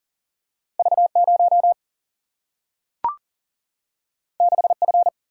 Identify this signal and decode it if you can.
Morse code, heard as a keyed tone, about 40 wpm, 700 Hz. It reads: V 0 6 F
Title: Single-Letter Single-Number 40wpm